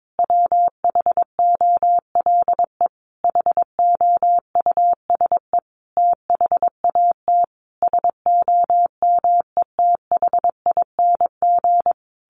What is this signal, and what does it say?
Morse code, heard as a keyed tone, about 22 wpm, 700 Hz. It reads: WHOLE HOUSE THAT SOMETHING
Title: Sentences from Top 1200 Words 22wpm